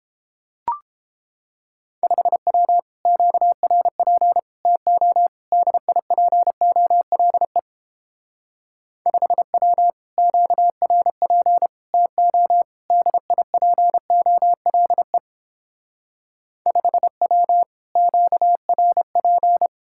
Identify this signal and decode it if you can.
Morse code, heard as a keyed tone, about 29 wpm, 700 Hz. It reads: 5W QRP TO DIPOLE 5W QRP TO DIPOLE 5W QRP